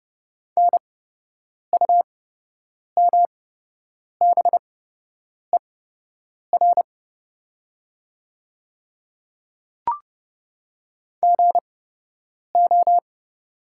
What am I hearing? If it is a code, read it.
Morse code, heard as a keyed tone, about 30 wpm, spaced as for 8 wpm, 700 Hz. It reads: NUMBER GO
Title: 8x Character Spacing - Top 100 Words - 30wpm